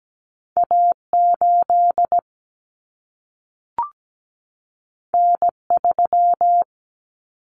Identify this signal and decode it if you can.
Morse code, heard as a keyed tone, about 17 wpm, 700 Hz. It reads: A8 N3
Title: Common Cut Numbers - 2-Digit 17wpm